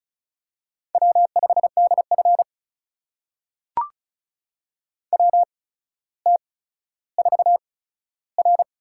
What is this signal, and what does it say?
Morse code, heard as a keyed tone, about 36 wpm, spaced as for 15 wpm, 700 Hz. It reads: W5DF WT4R
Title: USA Call Signs - 4 Characters - 8x Character Spacing - 35wpm